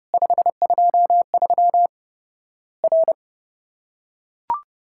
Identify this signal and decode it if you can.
Morse code, heard as a keyed tone, about 28 wpm, 700 Hz. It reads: 523 R
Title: Worked All Europe DX Contest - QTC Exchange - 30wpm